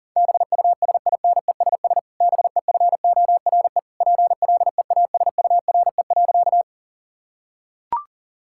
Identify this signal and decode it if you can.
Morse code, heard as a keyed tone, about 40 wpm, 700 Hz. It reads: BUSINESS BEFORE PLEASURE.
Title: Sentences from Top 1600 Words 40wpm